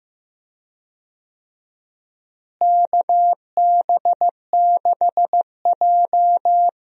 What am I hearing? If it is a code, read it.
Morse code, heard as a keyed tone, about 15 wpm, 700 Hz. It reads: KB6J
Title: USA Call Signs - 4 Characters - Repeated 2x - 15wpm